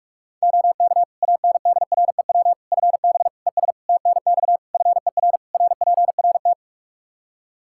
Morse code, heard as a keyed tone, about 46 wpm, 700 Hz. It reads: OK ANDREW FB ES TNX FER RPRT
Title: Practice QSO - Exchange 2 - Info - 45wpm